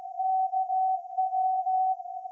Morse code, heard as a keyed tone, about 16 wpm, 750 Hz. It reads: KW
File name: iz7slz / VLF / SAQ 30-06-2024 0900